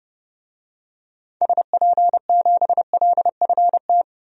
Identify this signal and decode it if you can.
Morse code, heard as a keyed tone, about 32 wpm, 700 Hz. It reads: SP7LFT
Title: DXCC-100 Call Signs - 6 Characters - 30wpm